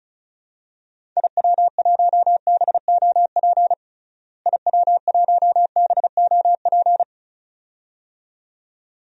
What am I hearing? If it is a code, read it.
Morse code, heard as a keyed tone, about 36 wpm, 700 Hz. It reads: IW1BOP IW1BOP
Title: DXCC-100 Call Signs - 6 Characters - Repeated 2x - 35wpm